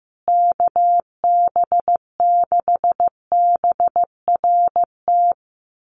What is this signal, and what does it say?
Morse code, heard as a keyed tone, about 15 wpm, 700 Hz. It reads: KB6BRT
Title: USA Call Signs Repeated 3x 15wpm